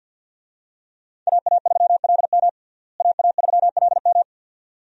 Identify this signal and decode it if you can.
Morse code, heard as a keyed tone, about 48 wpm, 700 Hz. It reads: AA3LM AA3LM
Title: USA Call Signs - 5 Characters - Repeated 2x - 50wpm